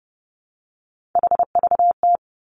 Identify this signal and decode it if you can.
Morse code, heard as a keyed tone, about 30 wpm, 700 Hz. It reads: HVT
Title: Three Challenging Letter-Number Combinations 30wpm